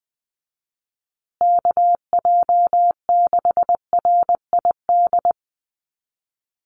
Morse code, heard as a keyed tone, about 20 wpm, 700 Hz. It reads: KJ6RID
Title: USA Call Signs Repeated 3x 20wpm